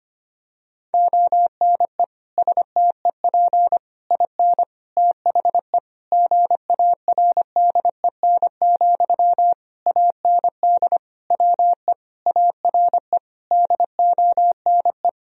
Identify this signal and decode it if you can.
Morse code, heard as a keyed tone, about 25 wpm, 690 Hz. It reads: ONE STEP IN THE GARDEN, AND WE ARE DONE
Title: Sentences from Top 1000 Words - Review Part 4 - 25wpm